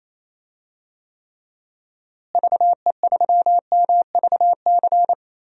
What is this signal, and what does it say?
Morse code, heard as a keyed tone, about 28 wpm, 700 Hz. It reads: VE3MVC
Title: DXCC-100 Call Signs - 6 Characters - Repeated 3x - 28wpm